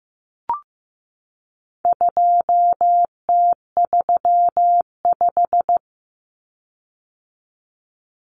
Morse code, heard as a keyed tone, about 15 wpm, 700 Hz. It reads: 2T35
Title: Common Cut Numbers - 4-Digit 15wpm